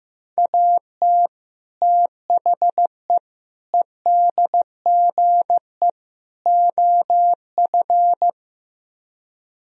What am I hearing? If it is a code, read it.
Morse code, heard as a keyed tone, about 15 wpm, 700 Hz. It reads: AT THE EDGE OF
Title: Speed-Racing - Sets of 4 Words 15wpm